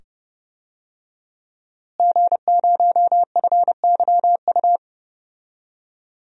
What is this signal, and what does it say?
Morse code, heard as a keyed tone, about 30 wpm, 700 Hz. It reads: G0FYU
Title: DXCC-100 Call Signs - 5 Characters - Repeated 2x - 30wpm